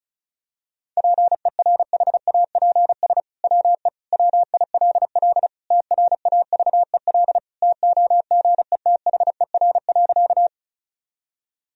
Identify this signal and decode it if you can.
Morse code, heard as a keyed tone, about 36 wpm, 700 Hz. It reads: PERHAPS WE WILL TRAVEL TOGETHER.
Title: Sentences from Top 1000 Words - Review Part 2 - 35wpm